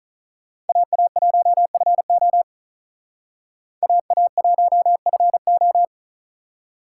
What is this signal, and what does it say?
Morse code, heard as a keyed tone, about 46 wpm, 700 Hz. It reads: AA1FO AA1FO